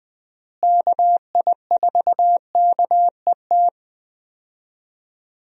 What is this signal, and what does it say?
Morse code, heard as a keyed tone, about 20 wpm, 700 Hz. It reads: KI4KET
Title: USA Call Signs - 6 Characters - Repeated 3x - 20wpm